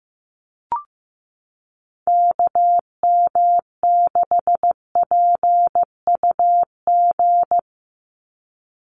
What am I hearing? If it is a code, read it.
Morse code, heard as a keyed tone, about 15 wpm, 700 Hz. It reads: KM6PUG
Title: USA Call Signs 15wpm